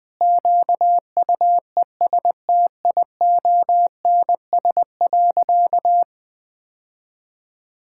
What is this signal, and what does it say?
Morse code, heard as a keyed tone, about 19 wpm, 700 Hz. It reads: QUESTIONS.
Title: Sentences with 4-Word Phrases - 20wpm